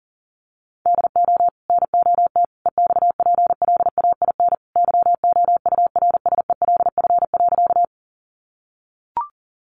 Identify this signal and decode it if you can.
Morse code, heard as a keyed tone, about 40 wpm, 700 Hz. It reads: DO NOT EXPLAIN YOURSELF.
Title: Sentences from Top 1100 Words 40wpm